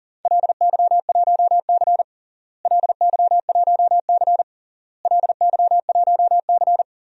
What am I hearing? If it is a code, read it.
Morse code, heard as a keyed tone, about 40 wpm, 700 Hz. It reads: LY1C LY1C LY1C
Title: DXCC-100 Call Signs - 4 Characters - Repeated 3x - 40wpm